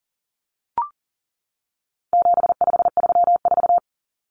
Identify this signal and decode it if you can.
Morse code, heard as a keyed tone, about 40 wpm, 700 Hz. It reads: 7534
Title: Common Cut Numbers - 4-Digit 40wpm